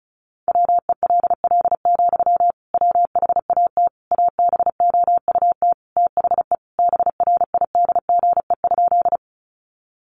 Morse code, heard as a keyed tone, about 34 wpm, 700 Hz. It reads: WELL, WHAT ABOUT THE BRIDGE?
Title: Sentences from Top 1200 Words 35wpm